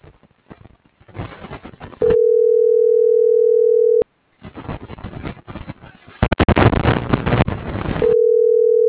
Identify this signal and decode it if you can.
Ringback tone (North America), 440 + 480 Hz.